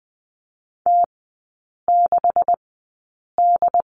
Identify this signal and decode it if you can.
Morse code, heard as a keyed tone, about 20 wpm, spaced as for 8 wpm, 700 Hz. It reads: T6D
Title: 127 Lesson 21e - Number 6 - Three Characters 20wpm